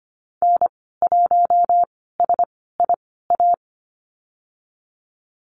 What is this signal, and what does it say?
Morse code, heard as a keyed tone, about 25 wpm, spaced as for 16 wpm, 700 Hz. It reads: N1SIA
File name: Lesson 15g - Number 5 - Call Signs 25wpm